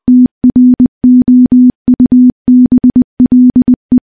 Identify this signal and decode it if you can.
Morse code, heard as a keyed tone, about 19 wpm, 250 Hz. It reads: TROUBLE
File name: morse / morse-training-20WPM